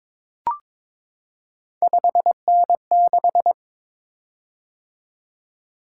Morse code, heard as a keyed tone, about 22 wpm, 700 Hz. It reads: 5N6
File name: practice / 22wpm / Three Letter-Number Combinations - 22wpm